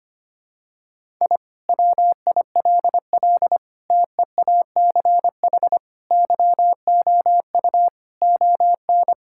Morse code, heard as a keyed tone, about 25 wpm, 700 Hz. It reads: I WILL TEACH YOU ON
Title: Sentences from Top 1000 Words - Review Part 6 - 25wpm